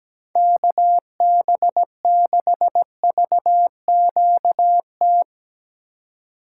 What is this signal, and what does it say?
Morse code, heard as a keyed tone, about 17 wpm, 700 Hz. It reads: KB6VQT
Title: USA Call Signs - 8x Character Spacing - 17wpm